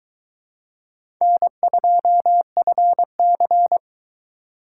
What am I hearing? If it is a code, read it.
Morse code, heard as a keyed tone, about 23 wpm, 700 Hz. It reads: N2FC